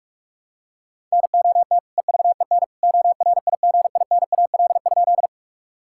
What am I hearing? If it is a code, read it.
Morse code, heard as a keyed tone, about 46 wpm, 700 Hz. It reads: NOT EVEN ORIGINAL?
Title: Sentences from Top 1000 Words - Review Part 6 - 45wpm